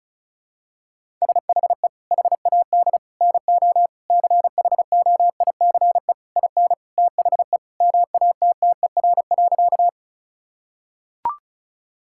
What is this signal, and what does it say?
Morse code, heard as a keyed tone, about 36 wpm, 700 Hz. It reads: SHE HAD NO CHOICE IN THE MATTER.
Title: Sentences from Top 1000 Words - Review Part 6 - 35wpm